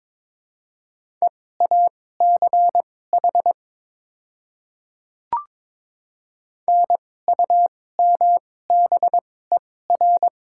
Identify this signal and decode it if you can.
Morse code, heard as a keyed tone, about 22 wpm, spaced as for 16 wpm, 700 Hz. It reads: EACH NUMBER
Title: 2x Character Spacing - Top 200 Words - 22wpm